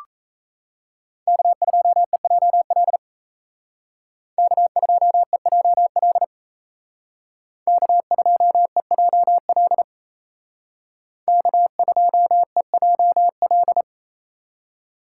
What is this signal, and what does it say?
Morse code, heard as a keyed tone, about 35 wpm, 700 Hz. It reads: K2EJL K2EJL K2EJL K2EJL